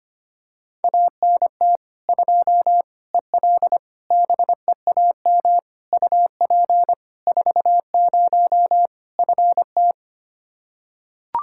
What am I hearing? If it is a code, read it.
Morse code, heard as a keyed tone, about 25 wpm, 700 Hz. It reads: ANT 2 EL BEAM UP 40 FT
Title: Practice QSO - Exchange 2 - Info - 25wpm